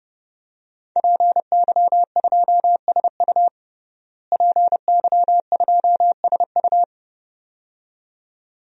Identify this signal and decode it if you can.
Morse code, heard as a keyed tone, about 30 wpm, 700 Hz. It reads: PY2SU PY2SU
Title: DXCC-100 Call Signs - 5 Characters - Repeated 2x - 30wpm